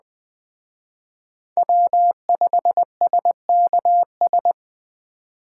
Morse code, heard as a keyed tone, about 20 wpm, 700 Hz. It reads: W5SKS